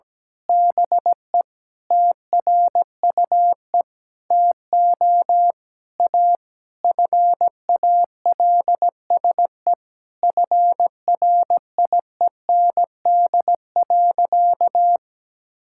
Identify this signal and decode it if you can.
Morse code, heard as a keyed tone, about 17 wpm, 700 Hz. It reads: BE TRUE TO A FALSE FRIEND.
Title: Sentences from Top 1600 Words 17wpm